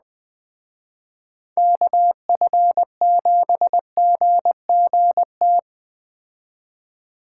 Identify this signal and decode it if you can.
Morse code, heard as a keyed tone, about 21 wpm, 700 Hz. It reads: KF7GGT
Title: Speed Racing - USA 2x3 Call Signs 17wpm